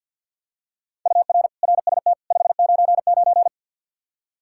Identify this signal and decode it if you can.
Morse code, heard as a keyed tone, about 49 wpm, 700 Hz. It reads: UR RST 599
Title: Practice QSO - Exchange 1 - Beginning - 50wpm